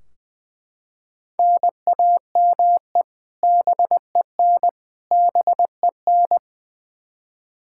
Morse code, heard as a keyed tone, about 21 wpm, 700 Hz. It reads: NAME BEN BEN
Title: Practice QSO - Exchange 1 - Beginning - 20wpm